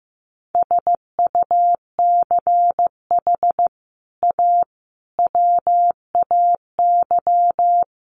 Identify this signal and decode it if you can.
Morse code, heard as a keyed tone, about 15 wpm, 700 Hz. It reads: SUCH A WAY
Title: Sets of 5 Words Encore2 15wpm